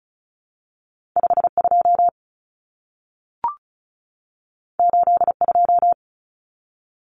Morse code, heard as a keyed tone, about 36 wpm, 700 Hz. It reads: 52 82
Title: Common Cut Numbers - 2-Digit 35wpm